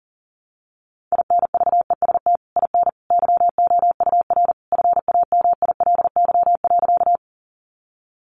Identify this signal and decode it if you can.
Morse code, heard as a keyed tone, about 40 wpm, 700 Hz. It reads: INVEST IN YOUR FAMILY.
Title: Sentences from Top 1100 Words 40wpm